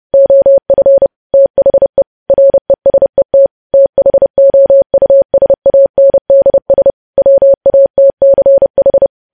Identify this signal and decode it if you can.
Morse code, heard as a keyed tone, about 30 wpm, 550 Hz. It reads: OF THE RESET THOUSANDS WATCH
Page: news headlines in morse code at 30 wpm